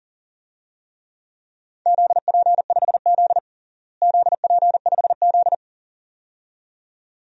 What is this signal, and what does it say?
Morse code, heard as a keyed tone, about 42 wpm, 690 Hz. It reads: ZP5Z ZP5Z
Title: DXCC-100 Call Signs - 4 Characters - Repeated 2x - 40wpm